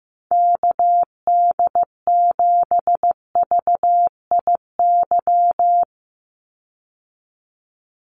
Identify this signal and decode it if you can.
Morse code, heard as a keyed tone, about 15 wpm, 700 Hz. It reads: KD7VIY
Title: USA Call Signs 15wpm